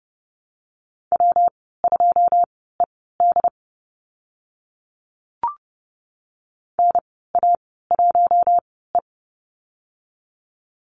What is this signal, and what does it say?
Morse code, heard as a keyed tone, about 30 wpm, spaced as for 17 wpm, 700 Hz. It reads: W2ED NA1E
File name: Lesson 19g - Letter w - Call Signs 30wpm